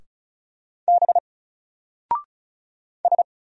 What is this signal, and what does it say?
Morse code, heard as a keyed tone, about 35 wpm, 700 Hz. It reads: B S
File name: Single-Letter-Number-35wpm